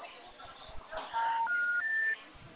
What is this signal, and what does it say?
Special information tone.
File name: out- -2030- - - .97